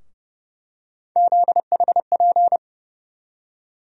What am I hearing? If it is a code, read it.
Morse code, heard as a keyed tone, about 32 wpm, 700 Hz. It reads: ZHP
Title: USA Call Sign Postfixes - 3 Letters 30wpm